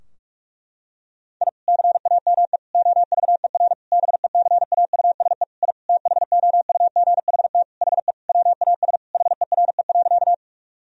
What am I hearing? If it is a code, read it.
Morse code, heard as a keyed tone, about 41 wpm, 700 Hz. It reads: I CAME OVER BECAUSE I THOUGHT HE WAS HERE.
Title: Sentences from Top 1000 Words - Review Part 3 - 45wpm